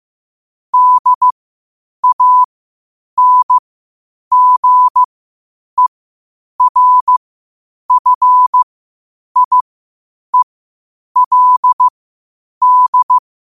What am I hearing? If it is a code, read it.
Morse code, heard as a keyed tone, about 15 wpm, spaced as for 8 wpm, 1000 Hz. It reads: DANGERFIELD
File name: qotdfiles / 8wpm